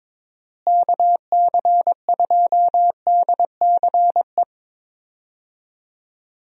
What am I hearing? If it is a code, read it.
Morse code, heard as a keyed tone, about 22 wpm, 700 Hz. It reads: KC2DCE
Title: USA Call Signs - 6 Characters - Repeated 3x - 22wpm